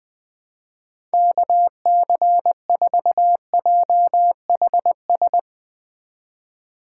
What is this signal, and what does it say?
Morse code, heard as a keyed tone, about 20 wpm, 700 Hz. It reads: KC4JHS
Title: USA Call Signs - 6 Characters - 2x Character Spacing - 20wpm